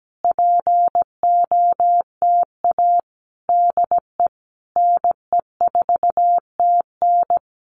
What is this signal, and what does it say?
Morse code, heard as a keyed tone, about 17 wpm, 700 Hz. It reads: POTA DE NE4TN
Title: Common QSO Phrases Encore 17wpm